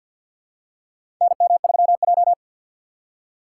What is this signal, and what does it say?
Morse code, heard as a keyed tone, about 49 wpm, 700 Hz. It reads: NM3J
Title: USA Call Signs - 4 Characters - 6x Character Spacing - 50wpm